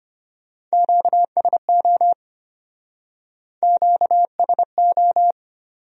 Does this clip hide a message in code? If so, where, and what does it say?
Yes: Morse code from 0.7 to 5.3 s, heard as a keyed tone, about 34 wpm, 700 Hz. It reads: QSO QSO